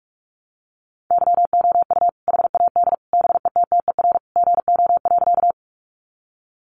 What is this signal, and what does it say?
Morse code, heard as a keyed tone, about 46 wpm, 700 Hz. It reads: YOU HAD BETTER GO.
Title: Sentences from Top 400 Words 45wpm